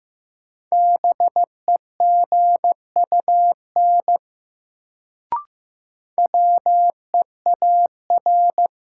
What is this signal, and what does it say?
Morse code, heard as a keyed tone, about 15 wpm, 700 Hz. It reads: BEGUN WEAR
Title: Sentences from Top 1000 Words - Review Part 6 - New Word Drill - 15wpm